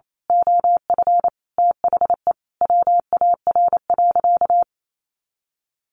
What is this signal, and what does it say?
Morse code, heard as a keyed tone, about 28 wpm, 700 Hz. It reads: OF THE WAR.
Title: Sentences from Top 1400 Words 28wpm